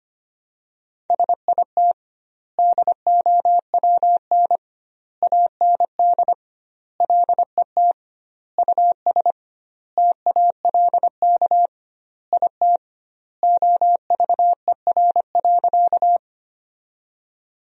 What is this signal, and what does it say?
Morse code, heard as a keyed tone, about 25 wpm, 700 Hz. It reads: SIT DOWN AND LET US TALK IT OVER.
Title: 2x Word-Spacing - Sentences from Top 500 Words - 25wpm